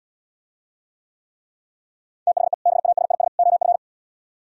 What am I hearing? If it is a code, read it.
Morse code, heard as a keyed tone, about 77 wpm, 700 Hz. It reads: THE BASIS OF